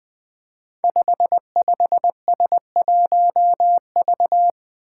Morse code, heard as a keyed tone, about 20 wpm, 700 Hz. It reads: 55S1V
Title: Five Challenging Letter-Number Combinations 20wpm